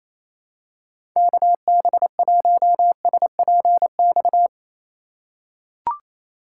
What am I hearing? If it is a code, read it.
Morse code, heard as a keyed tone, about 28 wpm, 700 Hz. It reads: KB1SPX
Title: USA Call Signs - 6 Characters - 4x Character Spacing - 28wpm